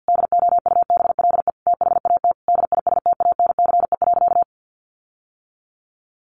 Morse code, heard as a keyed tone, about 50 wpm, 700 Hz. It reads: DOUBLE THAT DISTANCE.
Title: Sentences from Top 1400 Words 50wpm